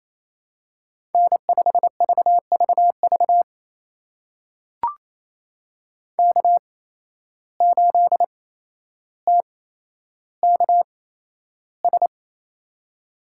Morse code, heard as a keyed tone, about 28 wpm, spaced as for 12 wpm, 700 Hz. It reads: N5VVV K8TKS
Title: USA Call Signs - 5 Characters - 8x Character Spacing - 28wpm